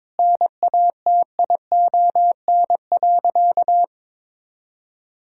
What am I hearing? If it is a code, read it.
Morse code, heard as a keyed tone, about 22 wpm, 700 Hz. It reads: NATION.
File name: practice / 22wpm / Sentences with 8-Word Phrases - 22wpm